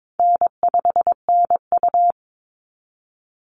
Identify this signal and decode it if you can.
Morse code, heard as a keyed tone, about 22 wpm, 700 Hz. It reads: N5NU
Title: Call Signs Repeated 3x - 22wpm